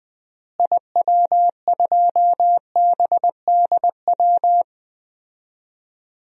Morse code, heard as a keyed tone, about 21 wpm, 700 Hz. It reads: IW2BDW
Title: DXCC-100 Call Signs - 6 Characters - Repeated 3x - 20wpm